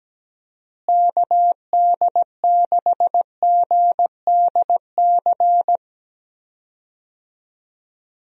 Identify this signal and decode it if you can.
Morse code, heard as a keyed tone, about 17 wpm, 700 Hz. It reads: KD6GDC
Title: USA Call Signs - 6 Characters - Repeated 2x - 17wpm